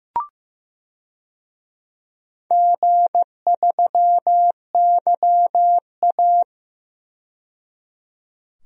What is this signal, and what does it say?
Morse code, heard as a keyed tone, about 15 wpm, 700 Hz. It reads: G3YA
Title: DXCC-100 Call Signs - 4 Characters - 15wpm